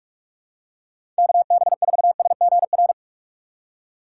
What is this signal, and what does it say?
Morse code, heard as a keyed tone, about 46 wpm, 700 Hz. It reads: KB4SGR
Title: USA Call Signs - 6 Characters - 6x Character Spacing - 45wpm